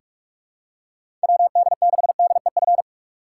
Morse code, heard as a keyed tone, about 46 wpm, 700 Hz. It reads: WD6DEF
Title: USA Call Signs - 6 Characters - 4x Character Spacing - 45wpm